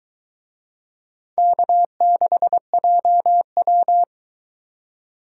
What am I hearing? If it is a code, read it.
Morse code, heard as a keyed tone, about 23 wpm, 700 Hz. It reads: K6JW